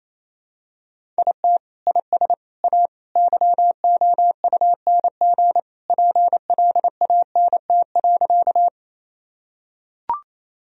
Morse code, heard as a keyed tone, about 28 wpm, 700 Hz. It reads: IT IS A YOUNG PLANT.
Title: Sentences from Top 1000 Words - Review Part 1 - 28wpm